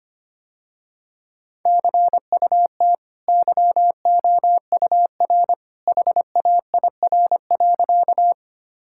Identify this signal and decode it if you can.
Morse code, heard as a keyed tone, about 25 wpm, 700 Hz. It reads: CUT YOUR HAIR.
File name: Easy Sentences with Hints - 25wpm